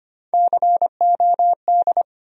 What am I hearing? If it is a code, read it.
Morse code, heard as a keyed tone, about 25 wpm, 700 Hz. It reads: COD
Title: USA Call Sign Postfixes - 3 Letters 25wpm